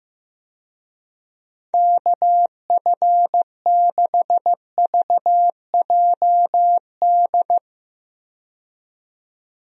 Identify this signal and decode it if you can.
Morse code, heard as a keyed tone, about 15 wpm, 700 Hz. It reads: KF6VJD
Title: USA Call Signs - 6 Characters - 15wpm